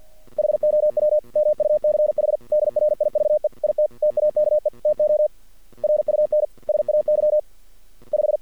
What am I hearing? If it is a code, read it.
Morse code, heard as a keyed tone, about 50 wpm, 620 Hz. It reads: HOW MUCH DRIVE IT TAKE TO RUN SAY 5
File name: archived / CRSnet